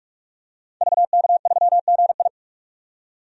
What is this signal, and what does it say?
Morse code, heard as a keyed tone, about 44 wpm, 690 Hz. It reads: VK3GI
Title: DXCC-100 Call Signs - 5 Characters - 45wpm